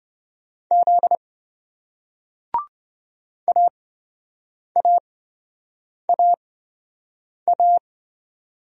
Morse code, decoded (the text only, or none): Z AAAA